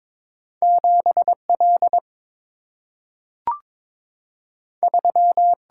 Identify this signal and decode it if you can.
Morse code, heard as a keyed tone, about 22 wpm, 700 Hz. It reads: 7 L 3
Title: Single-Letter Single-Number 22wpm